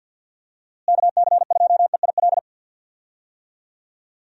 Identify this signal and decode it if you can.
Morse code, heard as a keyed tone, about 50 wpm, 700 Hz. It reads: KC2EIL
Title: USA Call Signs - 6 Characters - Repeated 2x - 50wpm